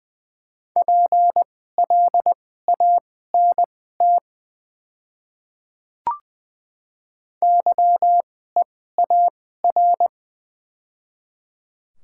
Morse code, heard as a keyed tone, about 20 wpm, spaced as for 14 wpm, 700 Hz. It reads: PLANT YEAR
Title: 2x Character Spacing - Top 300 Words - 20wpm